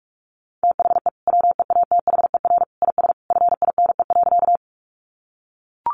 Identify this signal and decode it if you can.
Morse code, heard as a keyed tone, about 44 wpm, 700 Hz. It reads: THE WEATHER IS FINE.
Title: Sentences from Top 800 Words 45wpm